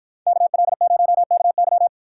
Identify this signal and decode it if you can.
Morse code, heard as a keyed tone, about 64 wpm, 700 Hz. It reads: KL0KY